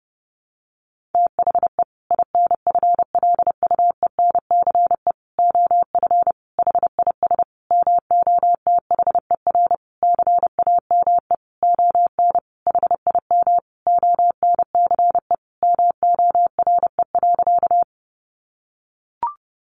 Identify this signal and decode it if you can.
Morse code, heard as a keyed tone, about 30 wpm, 700 Hz. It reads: THE INFLUENCE OF HIS MOTHER CAME ON HIM ONCE MORE.
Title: Sentences from Top 1100 Words 30wpm